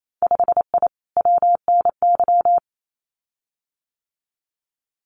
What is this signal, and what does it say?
Morse code, heard as a keyed tone, about 28 wpm, 700 Hz. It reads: 5I WNY
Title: Winter Field Day Exchanges Encore 28wpm